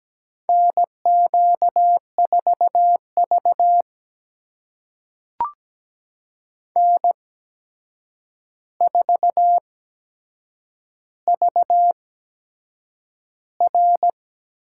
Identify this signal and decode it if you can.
Morse code, heard as a keyed tone, about 17 wpm, spaced as for 7 wpm, 700 Hz. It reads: NQ4V N4VR
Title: USA Call Signs - 4 Characters - 8x Character Spacing - 17wpm